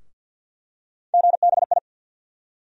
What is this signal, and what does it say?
Morse code, heard as a keyed tone, about 51 wpm, 700 Hz. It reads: GBI
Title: USA Call Sign Postfixes - 3 Letters 50wpm